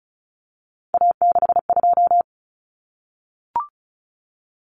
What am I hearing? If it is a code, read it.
Morse code, heard as a keyed tone, about 34 wpm, 700 Hz. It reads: A62
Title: Common Cut Numbers - 3-Digit 35wpm